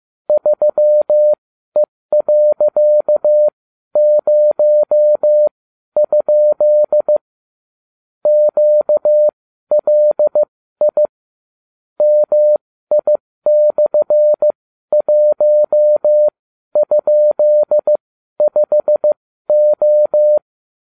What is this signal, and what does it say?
Morse code, heard as a keyed tone, about 15 wpm, spaced as for 12 wpm, 600 Hz. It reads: 3E.0? QLI MI/1?5O